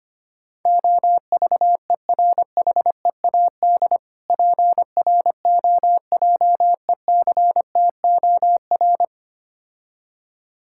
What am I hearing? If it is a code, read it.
Morse code, heard as a keyed tone, about 25 wpm, 700 Hz. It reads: OVERHEAD PROJECTOR
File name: practice / 25wpm / Difficult Words - Speed-Racing with Hints - 25wpm